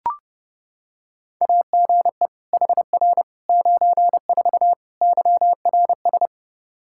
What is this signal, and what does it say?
Morse code, heard as a keyed tone, about 30 wpm, 700 Hz. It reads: AGE HR 94 YRS
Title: K1USN SST QSO Practice 30wpm